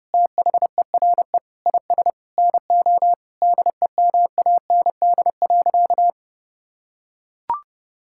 Morse code, heard as a keyed tone, about 30 wpm, 700 Hz. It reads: THERE IS NO DEMAND.
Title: Sentences from Top 1000 Words - Review Part 5 - 30wpm